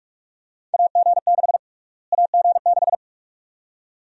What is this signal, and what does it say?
Morse code, heard as a keyed tone, about 46 wpm, 700 Hz. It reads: AG6 AG6
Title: USA Call Sign Prefixes - Repeated 2x - 45wpm